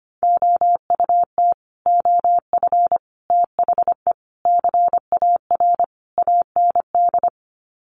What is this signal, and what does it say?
Morse code, heard as a keyed tone, about 25 wpm, 700 Hz. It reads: OUT OF THE CAR AND
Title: Sets of 5 Words Encore 25wpm